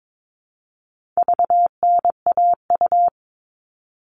Morse code, heard as a keyed tone, about 22 wpm, 700 Hz. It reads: VNAU
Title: All Cut Numbers - 4-Digit 22wpm